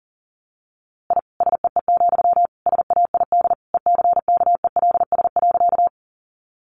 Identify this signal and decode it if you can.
Morse code, heard as a keyed tone, about 40 wpm, 700 Hz. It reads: I SEE, SAID ECKELS.
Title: A Sound Of Thunder 40wpm